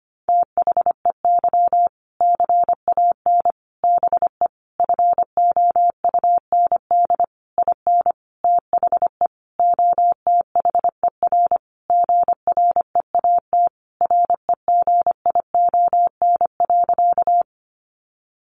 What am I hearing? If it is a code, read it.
Morse code, heard as a keyed tone, about 25 wpm, 700 Hz. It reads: THEY CAN BE FOUND IN THE OTHER GREAT REGION.
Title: Sentences from Top 1500 Words 25wpm